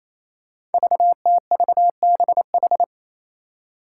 Morse code, heard as a keyed tone, about 28 wpm, 700 Hz. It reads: VTVBH
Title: Five Challenging Letter-Number Combinations 28wpm